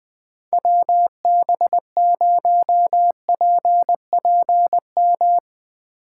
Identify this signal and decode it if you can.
Morse code, heard as a keyed tone, about 20 wpm, 700 Hz. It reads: WB0PPM